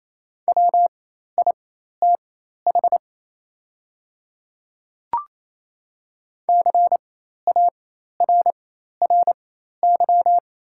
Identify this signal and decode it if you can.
Morse code, heard as a keyed tone, about 28 wpm, spaced as for 13 wpm, 690 Hz. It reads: WITH CARRY
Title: 4x Character Spacing - Top 300 Words - 28wpm